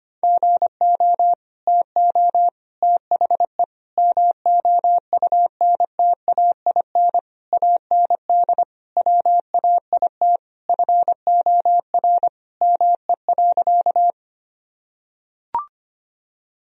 Morse code, heard as a keyed tone, about 25 wpm, 700 Hz. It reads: GO TO THE MOUNTAIN AND WAIT FOR ME.
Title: Sentences from Top 1000 Words - Review Part 1 - 25wpm